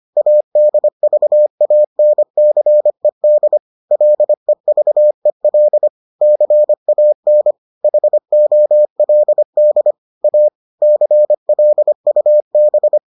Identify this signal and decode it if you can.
Morse code, heard as a keyed tone, about 25 wpm, 600 Hz. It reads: ADVANCED LEVEL CAN HOLD A CLUB